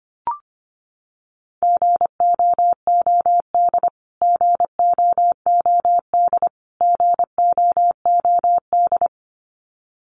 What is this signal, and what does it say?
Morse code, heard as a keyed tone, about 25 wpm, 700 Hz. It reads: GOOD GOOD GOOD
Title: Top 500 Words Repeated 25wpm